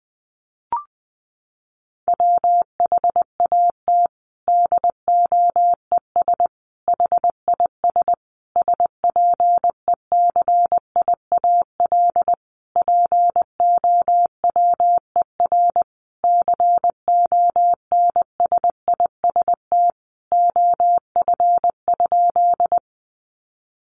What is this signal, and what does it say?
Morse code, heard as a keyed tone, about 20 wpm, 700 Hz. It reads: WHAT DOES HIS SPECIAL POWER CONSIST OF?
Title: Sentences from Top 1400 Words 20wpm